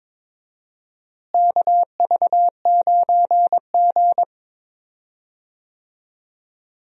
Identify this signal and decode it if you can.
Morse code, heard as a keyed tone, about 22 wpm, 700 Hz. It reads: KV9G
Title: USA Call Signs - 4 Characters - Repeated 3x - 22wpm